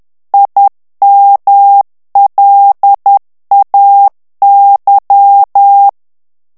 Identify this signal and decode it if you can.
Morse code, heard as a keyed tone, about 10 wpm, 800 Hz. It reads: IMLAY
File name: ANGLAIS texte clair 3 vitesse 10 mots minute